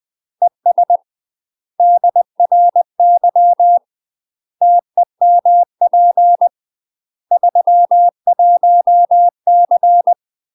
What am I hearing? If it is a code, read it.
Morse code, heard as a keyed tone, about 20 wpm, 700 Hz. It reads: ES DRY TEMP 31C
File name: UK MASTER QSOs TRAINING / 20WPM / UK MASTER QSOs 0073